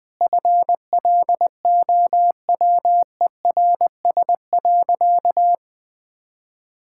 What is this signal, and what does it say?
Morse code, heard as a keyed tone, about 21 wpm, 700 Hz. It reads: FLOWERS.
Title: Sentences with 9-Word Phrases - Spoken Beginning - 20wpm